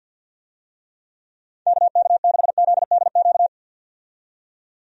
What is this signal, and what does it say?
Morse code, heard as a keyed tone, about 51 wpm, 700 Hz. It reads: KK6ZDQ
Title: USA Call Signs - 6 Characters - 50wpm